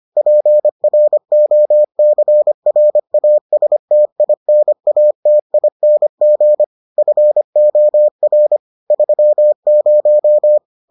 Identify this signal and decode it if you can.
Morse code, heard as a keyed tone, about 25 wpm, 600 Hz. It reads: PROCRASTINATING FOR 30